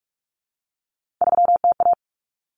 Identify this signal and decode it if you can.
Morse code, heard as a keyed tone, about 44 wpm, 700 Hz. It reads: 3TA
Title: Common Cut Numbers - 3-Digit 45wpm